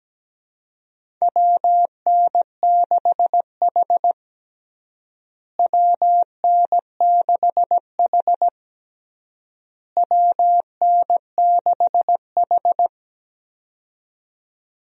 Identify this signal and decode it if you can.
Morse code, heard as a keyed tone, about 17 wpm, 700 Hz. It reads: WN6H WN6H WN6H
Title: USA Call Signs - 4 Characters - Repeated 3x - 17wpm